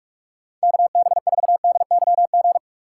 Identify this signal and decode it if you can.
Morse code, heard as a keyed tone, about 44 wpm, 700 Hz. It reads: KB4DYG